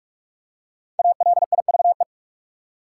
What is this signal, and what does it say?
Morse code, heard as a keyed tone, about 47 wpm, 700 Hz. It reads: ALIVE
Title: Five Letter Words - 45wpm